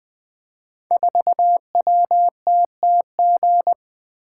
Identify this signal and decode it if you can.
Morse code, heard as a keyed tone, about 20 wpm, 700 Hz. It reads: 4WTTG
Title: Code Groups - 5 Letter:Numbers - 20wpm